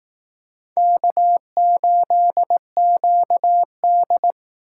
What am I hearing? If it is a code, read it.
Morse code, heard as a keyed tone, about 19 wpm, 700 Hz. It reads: K8QD